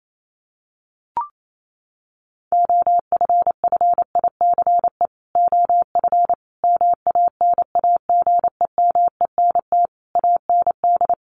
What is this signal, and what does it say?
Morse code, heard as a keyed tone, about 28 wpm, 700 Hz. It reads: OFFICE OF MANAGEMENT AND
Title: Sets of 5 Words Encore2 28wpm